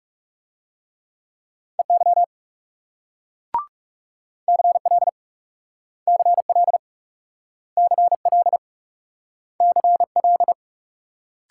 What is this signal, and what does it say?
Morse code, heard as a keyed tone, about 38 wpm, 700 Hz. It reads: EY CL CL CL CL